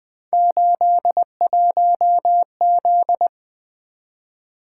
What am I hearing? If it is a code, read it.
Morse code, heard as a keyed tone, about 20 wpm, 700 Hz. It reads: 81Z
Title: Three Letter-Number Combinations - 20wpm